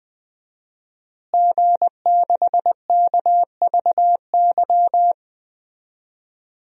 Morse code, heard as a keyed tone, about 20 wpm, 700 Hz. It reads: G6KVY
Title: DXCC-100 Call Signs - 5 Characters - Repeated 2x - 20wpm